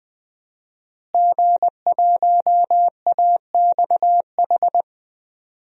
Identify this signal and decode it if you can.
Morse code, heard as a keyed tone, about 20 wpm, 700 Hz. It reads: G1AXH
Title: DXCC-100 Call Signs - 5 Characters - Repeated 2x - 20wpm